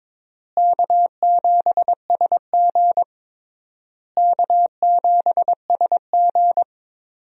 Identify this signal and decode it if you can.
Morse code, heard as a keyed tone, about 22 wpm, 700 Hz. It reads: K7SG K7SG
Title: USA Call Signs - 4 Characters - Repeated 3x - 22wpm